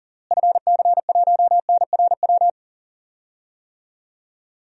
Morse code, heard as a keyed tone, about 40 wpm, 700 Hz. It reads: FC1NRW
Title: DXCC-100 Call Signs - 6 Characters - Repeated 3x - 40wpm